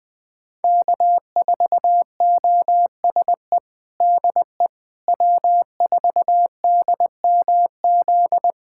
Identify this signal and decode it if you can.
Morse code, heard as a keyed tone, about 21 wpm, 700 Hz. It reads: K4OSE DE W4DMZ
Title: USA Callsign DE Callsign - 5 Characters - 20wpm